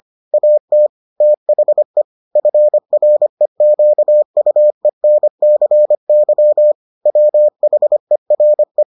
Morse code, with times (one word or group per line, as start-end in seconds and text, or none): 0.3-0.9 AT
1.2-2.0 THE
2.4-6.7 FREQUENCY
7.1-8.8 WHERE